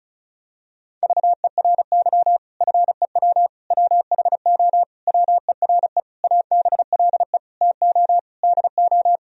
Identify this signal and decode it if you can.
Morse code, heard as a keyed tone, about 36 wpm, 700 Hz. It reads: VERY FEW WHO WERE ABLE TO DO
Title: Sentences from Top 1000 Words - Review Part 2 - 35wpm